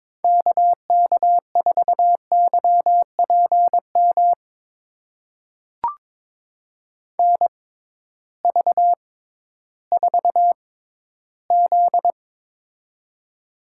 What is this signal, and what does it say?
Morse code, heard as a keyed tone, about 22 wpm, spaced as for 13 wpm, 700 Hz. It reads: KK4YPM NV4Z
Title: USA Call Signs - 6x Character Spacing - 22wpm